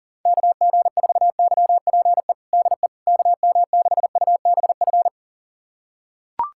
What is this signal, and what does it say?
Morse code, heard as a keyed tone, about 40 wpm, 700 Hz. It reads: KG4YPE DE KM6UBF
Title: USA Callsign DE Callsign 40wpm